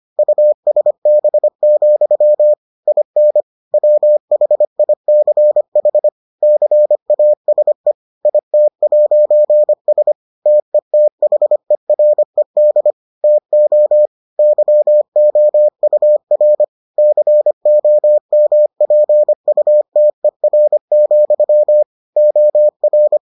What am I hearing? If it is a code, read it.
Morse code, heard as a keyed tone, about 25 wpm, 600 Hz. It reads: USB, IN WHICH CASE IT'S TETHERED TO YOUR COMPUTER, OR